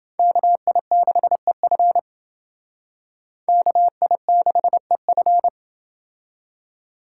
Morse code, heard as a keyed tone, about 28 wpm, 700 Hz. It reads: KI6EF KI6EF